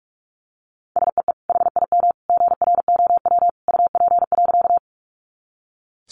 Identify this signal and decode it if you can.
Morse code, heard as a keyed tone, about 44 wpm, 700 Hz. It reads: SEE HIM GROW UP.
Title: Sentences from Top 300 Words 45wpm